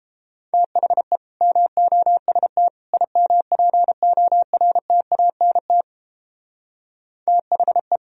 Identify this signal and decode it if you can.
Morse code, heard as a keyed tone, about 29 wpm, 700 Hz. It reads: THE MOST IMPORTANT THE